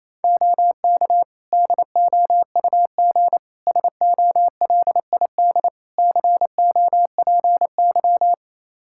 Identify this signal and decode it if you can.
Morse code, heard as a keyed tone, about 28 wpm, 700 Hz. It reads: OK DOUG SOLID COPY
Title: K1USN SST QSO Practice 28wpm